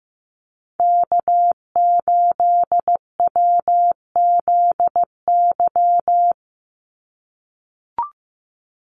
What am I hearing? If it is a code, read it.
Morse code, heard as a keyed tone, about 15 wpm, 700 Hz. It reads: K8WZY
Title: USA Call Signs 15wpm